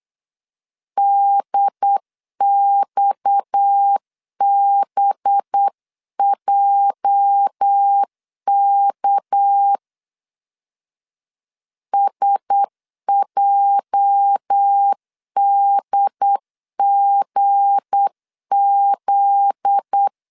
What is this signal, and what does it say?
Morse code, heard as a keyed tone, about 8 wpm, 800 Hz. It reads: DXBJK SJDGZ
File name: LEZIONI / 8 YJ45 / 8 YJ 8wpm 3313